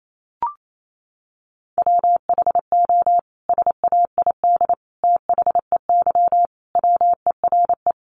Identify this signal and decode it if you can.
Morse code, heard as a keyed tone, about 28 wpm, 700 Hz. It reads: WHO SAID THEY WERE
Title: Sentences from Top 700 Words 28wpm